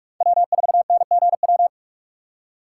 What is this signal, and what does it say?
Morse code, heard as a keyed tone, about 46 wpm, 700 Hz. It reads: W4NGW
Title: USA Call Signs - 5 Characters - 3x Character Spacing - 45wpm